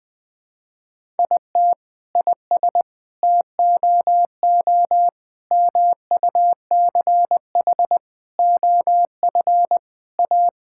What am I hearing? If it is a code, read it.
Morse code, heard as a keyed tone, about 21 wpm, 700 Hz. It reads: IT IS TOO MUCH OF A
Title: Sentences from Top 1000 Words - Review Part 5 - 20wpm